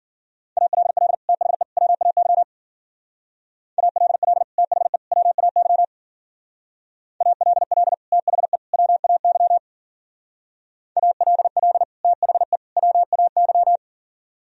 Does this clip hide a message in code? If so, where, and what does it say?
Yes: Morse code from 0.6 to 13.8 s, heard as a keyed tone, about 51 wpm, 700 Hz. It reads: ALL THE WAY ALL THE WAY ALL THE WAY ALL THE WAY